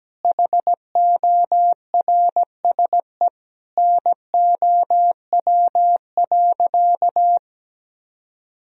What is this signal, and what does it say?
Morse code, heard as a keyed tone, about 17 wpm, 700 Hz. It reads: HORSE NOW.
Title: Sentences from Top 1000 Words - Review Part 1 - 17wpm